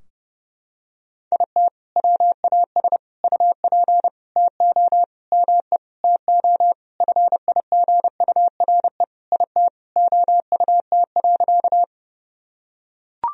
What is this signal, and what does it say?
Morse code, heard as a keyed tone, about 30 wpm, 700 Hz. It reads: IT WAS UP TO ME TO FIGURE IT OUT.
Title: Sentences from Top 1000 Words - Review Part 2 - 30wpm